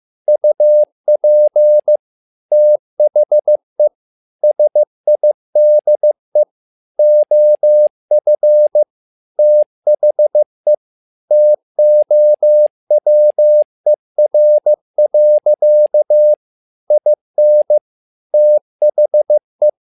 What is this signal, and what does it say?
Morse code, heard as a keyed tone, about 15 wpm, 600 Hz. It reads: UP THE SIDE OF THE TOWER. IN THE
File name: learning center / tower trouble / tower trouble 15wpm